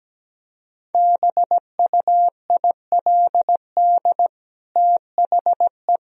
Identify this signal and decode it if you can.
Morse code, heard as a keyed tone, about 17 wpm, 700 Hz. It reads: BUILD THE
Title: Sentences from Top 1000 Words - Review Part 1 - 17wpm